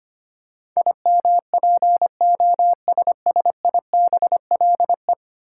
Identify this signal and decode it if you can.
Morse code, heard as a keyed tone, about 25 wpm, 700 Hz. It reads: IMPOSSIBLE
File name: 10 Letter Words - First 3 Letters Given - 25wpm